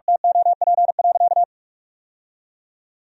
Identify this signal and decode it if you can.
Morse code, heard as a keyed tone, about 46 wpm, 700 Hz. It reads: TOP.
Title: Sentences from Top 1000 Words - Review Part 2 - 45wpm